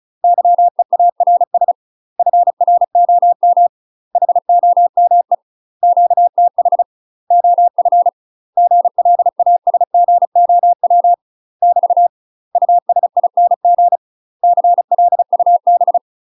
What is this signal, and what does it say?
Morse code, heard as a keyed tone, about 36 wpm, 700 Hz. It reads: YEARS FROM HOME QTH OF GLASGOW <BT> USING CLUB
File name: UK MASTER QSOs TRAINING / 35WPM / UK MASTER QSOs 0029